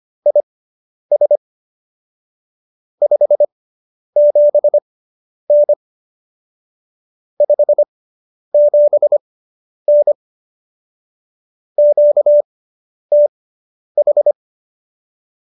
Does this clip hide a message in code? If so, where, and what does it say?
Yes: Morse code from 0.3 to 14.3 s, heard as a keyed tone, about 25 wpm, spaced as for 10 wpm, 590 Hz. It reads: IS 57N 57N QTH